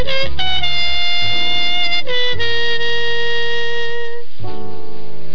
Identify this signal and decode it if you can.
Dial-up modem handshake.